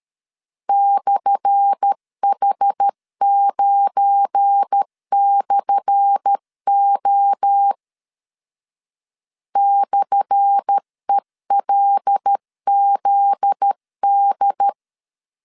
Morse code, heard as a keyed tone, about 13 wpm, 800 Hz. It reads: /H9/O /ELZD
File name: LEZIONI / 10 SIGNS / 10B INT 12wpm 2414